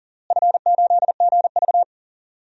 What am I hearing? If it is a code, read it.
Morse code, heard as a keyed tone, about 40 wpm, 700 Hz. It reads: F8GV
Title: DXCC-100 Call Signs - 4 Characters - Repeated 2x - 40wpm